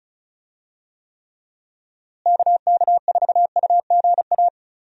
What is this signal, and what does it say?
Morse code, heard as a keyed tone, about 36 wpm, 700 Hz. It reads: KK4UGA
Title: USA Call Signs - 6 Characters - Repeated 3x - 35wpm